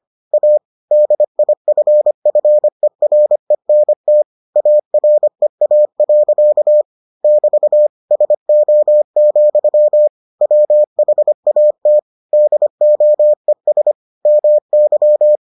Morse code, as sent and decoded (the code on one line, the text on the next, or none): .- / -.. .. ..-. ..-. .-. -. - / .- .-. .- .-.-.- / -...- / ... --- --..-- / .-- .... .- - / -.. --- . ... / -- -.--
A DIFFERENT AREA. <BT> SO, WHAT DOES MY